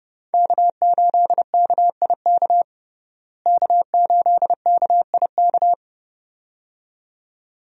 Morse code, heard as a keyed tone, about 32 wpm, 700 Hz. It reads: K8KIK K8KIK
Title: USA Call Signs - 5 Characters - Repeated 2x - 30wpm